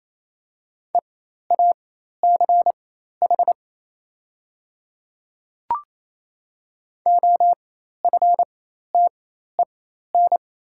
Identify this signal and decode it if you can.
Morse code, heard as a keyed tone, about 28 wpm, spaced as for 13 wpm, 700 Hz. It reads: EACH OFTEN
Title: 4x Character Spacing - Top 300 Words - 28wpm